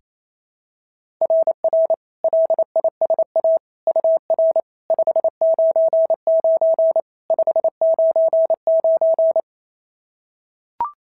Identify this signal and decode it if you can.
Morse code, heard as a keyed tone, about 27 wpm, 690 Hz. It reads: RR LISA UR 599 599
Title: SOTA QSO Practice 28wpm